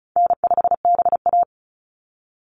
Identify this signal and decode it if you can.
Morse code, heard as a keyed tone, about 34 wpm, 700 Hz. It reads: N5BA
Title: Call Signs Repeated 3x - 35wpm